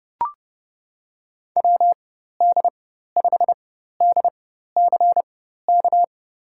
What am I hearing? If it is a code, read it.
Morse code, heard as a keyed tone, about 30 wpm, spaced as for 14 wpm, 700 Hz. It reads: WD5DCK
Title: USA Call Signs - 4x Character Spacing - 30wpm